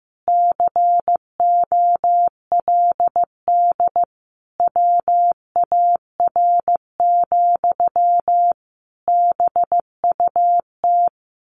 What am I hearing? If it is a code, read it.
Morse code, heard as a keyed tone, about 15 wpm, 700 Hz. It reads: COLD WAR, BUT
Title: Sentences from Top 1200 Words 15wpm